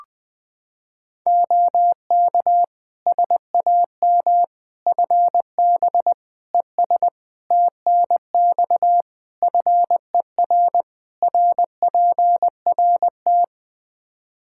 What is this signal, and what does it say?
Morse code, heard as a keyed tone, about 20 wpm, 700 Hz. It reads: OK SAM FB ES TNX FER RPRT
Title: Practice QSO - Exchange 2 - Info - 20wpm